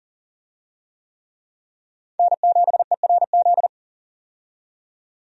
Morse code, heard as a keyed tone, about 40 wpm, 700 Hz. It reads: N7ERZ
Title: USA Call Signs - 5 Characters - 40wpm